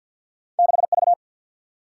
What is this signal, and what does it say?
Morse code, heard as a keyed tone, about 46 wpm, 700 Hz. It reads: 6V
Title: Single-Letter Single-Number 50wpm